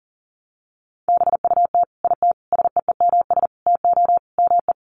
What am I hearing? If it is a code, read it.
Morse code, heard as a keyed tone, about 40 wpm, 700 Hz. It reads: BUT IT SEEMS TO ME
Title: Sets of 5 Words Encore2 40wpm